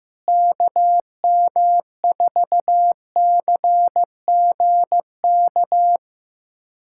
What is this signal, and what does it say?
Morse code, heard as a keyed tone, about 15 wpm, 700 Hz. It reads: KM4CGK
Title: USA Callsign DE Callsign - 6 Characters - 15wpm